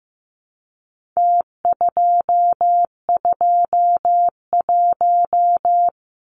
Morse code, heard as a keyed tone, about 15 wpm, 700 Hz. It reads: T221
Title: Numbers - Contest Style - 4-Digit 15wpm